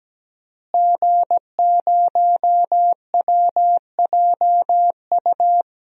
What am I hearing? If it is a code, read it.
Morse code, heard as a keyed tone, about 17 wpm, 700 Hz. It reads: G0WJU
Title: DXCC-100 Call Signs - 5 Characters - Repeated 3x - 17wpm